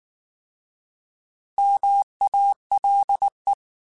Morse code, heard as a keyed tone, about 19 wpm, 800 Hz. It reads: MALE